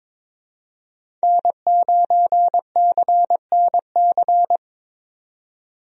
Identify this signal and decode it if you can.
Morse code, heard as a keyed tone, about 22 wpm, 700 Hz. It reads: N9CNC
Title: USA Call Signs - 5 Characters - 22wpm